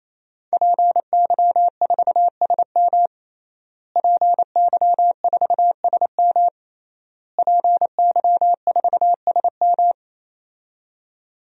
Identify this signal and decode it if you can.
Morse code, heard as a keyed tone, about 28 wpm, 700 Hz. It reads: PY4SM PY4SM PY4SM
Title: DXCC-100 Call Signs - 5 Characters - Repeated 3x - 28wpm